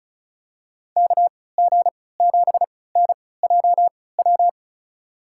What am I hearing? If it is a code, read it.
Morse code, heard as a keyed tone, about 36 wpm, spaced as for 20 wpm, 700 Hz. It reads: KG7NJW
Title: USA Call Signs - 6 Characters - 3x Character Spacing - 35wpm